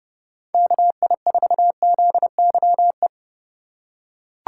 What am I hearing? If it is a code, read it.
Morse code, heard as a keyed tone, about 30 wpm, 700 Hz. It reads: KI4ZYE
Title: USA Call Signs - 3x Character Spacing - 30wpm